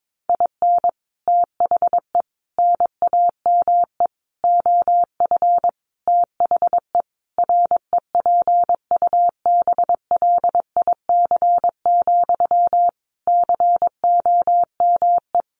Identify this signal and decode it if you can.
Morse code, heard as a keyed tone, about 22 wpm, 700 Hz. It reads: IN THE NAME OF THE REPUBLIC, COME
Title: Sentences from Top 1500 Words 22wpm